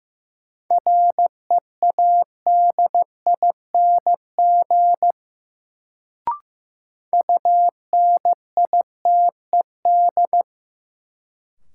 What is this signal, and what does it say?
Morse code, heard as a keyed tone, about 15 wpm, 700 Hz. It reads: READING UNITED
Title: Sentences from Top 600 Words - New Word Drill - 15wpm